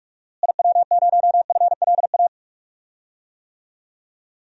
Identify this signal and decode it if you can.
Morse code, heard as a keyed tone, about 46 wpm, 700 Hz. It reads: IW0FLA
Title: DXCC-100 Call Signs - 6 Characters - Repeated 3x - 45wpm